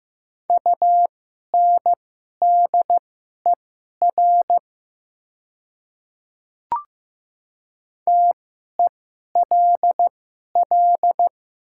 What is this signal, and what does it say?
Morse code, heard as a keyed tone, about 15 wpm, spaced as for 11 wpm, 690 Hz. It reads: UNDER TELL
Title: 2x Character Spacing - Top 200 Words - 15wpm